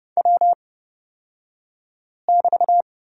Morse code, heard as a keyed tone, about 30 wpm, spaced as for 5 wpm, 700 Hz. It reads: W<BT>
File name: Lesson 36d - BT - Two Characters 30wpm